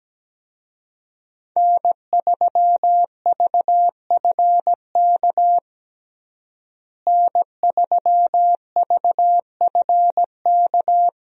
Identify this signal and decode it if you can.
Morse code, heard as a keyed tone, about 17 wpm, 700 Hz. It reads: N3VFK N3VFK
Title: USA Call Signs - 5 Characters - Repeated 2x - 17wpm